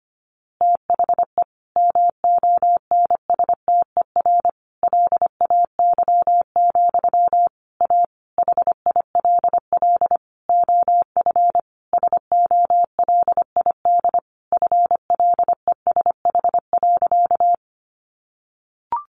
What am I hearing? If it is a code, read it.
Morse code, heard as a keyed tone, about 25 wpm, 700 Hz. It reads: THE MONSTER LAY, A HILL OF SOLID FLESH.
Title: A Sound Of Thunder 25wpm